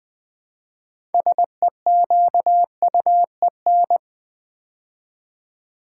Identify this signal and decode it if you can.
Morse code, heard as a keyed tone, about 20 wpm, 700 Hz. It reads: SEQUEN
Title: Build A Word - 3 Letters At A Time - 8-Letter Words - 20wpm